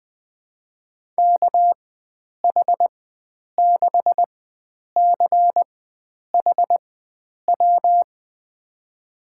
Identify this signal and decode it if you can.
Morse code, heard as a keyed tone, about 20 wpm, spaced as for 9 wpm, 700 Hz. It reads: KH6CHW
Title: USA Call Signs - 6 Characters - 4x Character Spacing - 20wpm